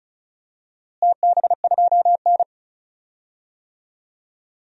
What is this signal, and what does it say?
Morse code, heard as a keyed tone, about 36 wpm, 700 Hz. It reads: TB2N
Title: DXCC-100 Call Signs - 4 Characters - Repeated 3x - 35wpm